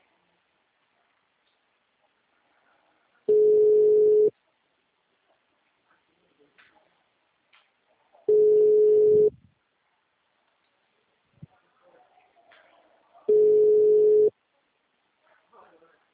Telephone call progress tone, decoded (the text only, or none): ringback tone